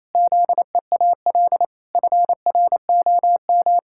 Morse code, heard as a keyed tone, about 28 wpm, 700 Hz. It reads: ZEAL FROM
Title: Pangram Challenge v2 - 28wpm